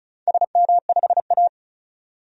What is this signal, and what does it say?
Morse code, heard as a keyed tone, about 36 wpm, 700 Hz. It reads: SM5A
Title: DXCC-100 Call Signs - 4 Characters - 35wpm